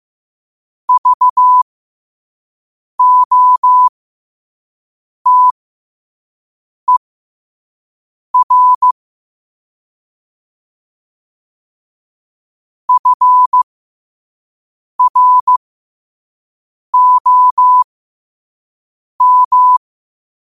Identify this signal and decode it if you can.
Morse code, heard as a keyed tone, about 15 wpm, spaced as for 5 wpm, 1000 Hz. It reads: VOTER FROM
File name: qotdfiles / 5wpm